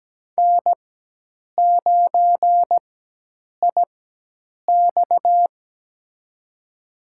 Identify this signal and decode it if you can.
Morse code, heard as a keyed tone, about 17 wpm, spaced as for 8 wpm, 700 Hz. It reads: N9IX
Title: USA Call Signs - 4 Characters - 4x Character Spacing - 17wpm